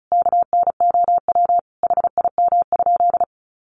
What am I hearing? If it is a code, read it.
Morse code, heard as a keyed tone, about 34 wpm, 700 Hz. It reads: KNOW HIM?
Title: Sentences from Top 100 Words 35wpm